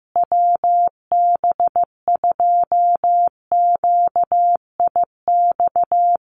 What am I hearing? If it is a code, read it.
Morse code, heard as a keyed tone, about 15 wpm, 700 Hz. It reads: WB2QIX
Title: Common QSO Phrases Encore 15wpm